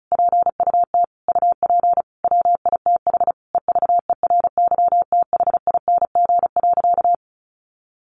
Morse code, heard as a keyed tone, about 34 wpm, 700 Hz. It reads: PUT UP WITH EVERYTHING.
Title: Sentences from Top 1100 Words 35wpm